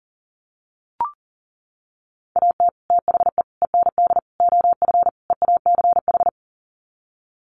Morse code, heard as a keyed tone, about 37 wpm, 700 Hz. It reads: AT THE END OF EACH
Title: Sets of 5 Words Encore 40wpm